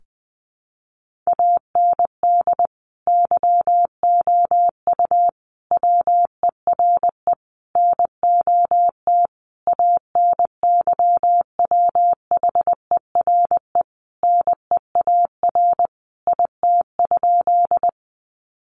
Morse code, heard as a keyed tone, about 20 wpm, 700 Hz. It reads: AND YOU WERE NOT ANYWHERE NEAR IT?
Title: Sentences from Top 1400 Words 20wpm